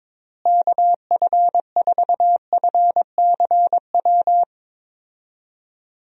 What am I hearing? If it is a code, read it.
Morse code, heard as a keyed tone, about 22 wpm, 700 Hz. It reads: KF4FCW
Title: USA Call Signs - 6 Characters - Repeated 3x - 22wpm